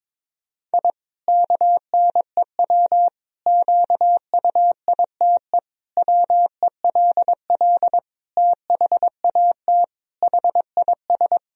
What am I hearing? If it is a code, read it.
Morse code, heard as a keyed tone, about 22 wpm, 700 Hz. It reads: I KNEW QUITE WELL THAT HIS
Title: Sentences from Top 1000 Words - Review Part 3 - 22wpm